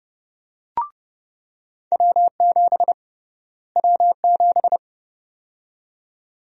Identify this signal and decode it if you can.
Morse code, heard as a keyed tone, about 31 wpm, 700 Hz. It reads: W7 W7
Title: USA Call Sign Prefixes - Repeated 2x - 30wpm